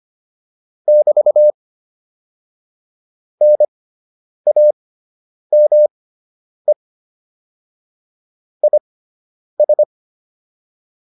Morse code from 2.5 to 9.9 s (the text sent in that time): NAME IS